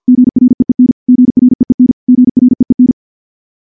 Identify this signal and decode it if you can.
Morse code, heard as a keyed tone, about 51 wpm, 250 Hz. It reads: GREEN GREEN GREEN
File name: morse / morse-training-50WPM